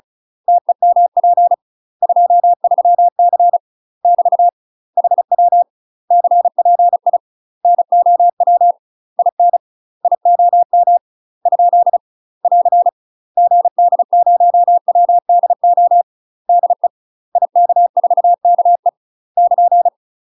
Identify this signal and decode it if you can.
Morse code, heard as a keyed tone, about 34 wpm, 700 Hz. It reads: TEMP 23C <BT> HW CPI NOW IN IOM ? <AR> GD0WDO DE IK4KE <KN>
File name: UK MASTER QSOs TRAINING / 35WPM / UK MASTER QSOs 0034